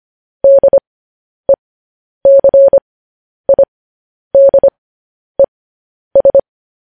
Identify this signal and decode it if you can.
Morse code, heard as a keyed tone, about 25 wpm, spaced as for 10 wpm, 550 Hz. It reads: DECIDES
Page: news-headlines-in-morse-code-at-10-wpm